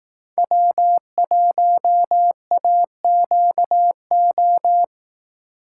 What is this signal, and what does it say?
Morse code, heard as a keyed tone, about 18 wpm, 700 Hz. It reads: W1AQO
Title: Speed Racing - USA 1x3 Call Signs 15wpm